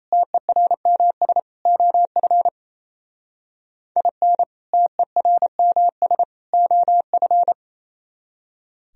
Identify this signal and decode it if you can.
Morse code, heard as a keyed tone, about 36 wpm, 700 Hz. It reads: TERMS OF IN TERMS OF